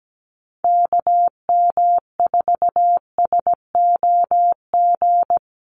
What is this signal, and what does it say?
Morse code, heard as a keyed tone, about 17 wpm, 700 Hz. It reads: KM4SOG
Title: USA Call Signs Repeated 3x 17wpm